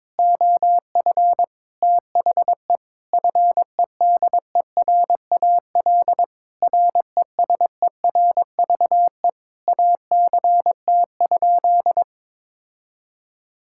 Morse code, heard as a keyed tone, about 22 wpm, 700 Hz. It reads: OF THE FEDERAL RESERVE ACT?
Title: Sentences from Top 1000 Words - Review Part 6 - 22wpm